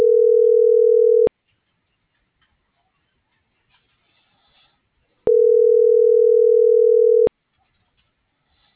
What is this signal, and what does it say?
Ringback tone (North America), 440 + 480 Hz.